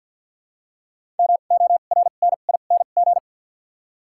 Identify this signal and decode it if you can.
Morse code, heard as a keyed tone, about 53 wpm, spaced as for 37 wpm, 700 Hz. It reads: MORNING
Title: 2x Character Spacing - Top 500 Words - 50wpm